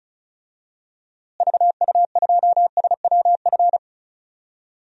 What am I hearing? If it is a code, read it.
Morse code, heard as a keyed tone, about 36 wpm, 700 Hz. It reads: VU2SWF
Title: DXCC-100 Call Signs - 6 Characters - Repeated 2x - 35wpm